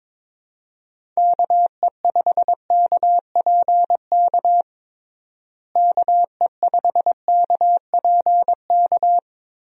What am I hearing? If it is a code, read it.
Morse code, heard as a keyed tone, about 22 wpm, 700 Hz. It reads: KE5KPK KE5KPK
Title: USA Call Signs - 6 Characters - Repeated 2x - 22wpm